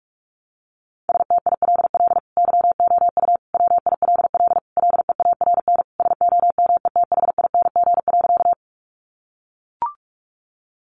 Morse code, heard as a keyed tone, about 44 wpm, 700 Hz. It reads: STILL YOU WILL LEARN SOMETHING.
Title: Sentences from Top 600 Words 45wpm